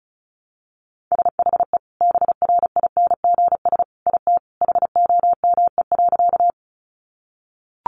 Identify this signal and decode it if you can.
Morse code, heard as a keyed tone, about 34 wpm, 700 Hz. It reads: SHE BRINGS IT HOME.
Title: Easy Phrases 35wpm